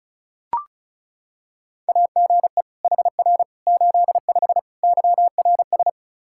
Morse code, heard as a keyed tone, about 34 wpm, 700 Hz. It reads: AGE HR 85 YRS
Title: Practice QSO - Exchange 3 - More Info - 35wpm